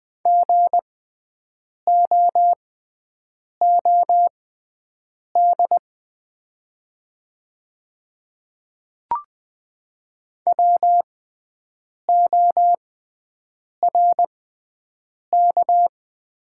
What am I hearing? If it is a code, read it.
Morse code, heard as a keyed tone, about 20 wpm, spaced as for 7 wpm, 700 Hz. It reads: GOOD WORK
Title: 6x Character Spacing - Top 200 Words - 20wpm